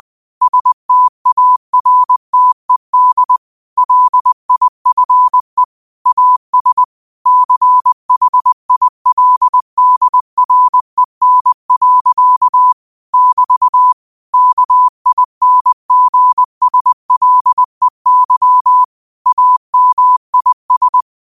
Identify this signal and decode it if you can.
Morse code, heard as a keyed tone, about 20 wpm, 1000 Hz. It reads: STARTED LIFE AS CHILDREN. <BT> KINGSLEY AMIS
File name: qotdfiles / 20wpm